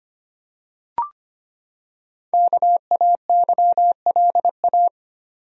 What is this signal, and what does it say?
Morse code, heard as a keyed tone, about 25 wpm, 700 Hz. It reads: KAYLA
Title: USA Common Names - Max 5 Letters - 25wpm